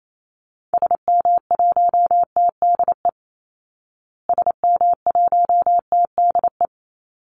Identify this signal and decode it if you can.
Morse code, heard as a keyed tone, about 28 wpm, 700 Hz. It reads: SM1TDE SM1TDE
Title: Call Signs Repeated 3x - 28wpm